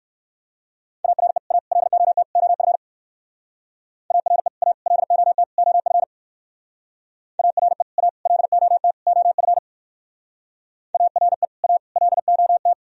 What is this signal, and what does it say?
Morse code, heard as a keyed tone, about 62 wpm, 700 Hz. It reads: ARE A LOT OF ARE A LOT OF ARE A LOT OF ARE A LOT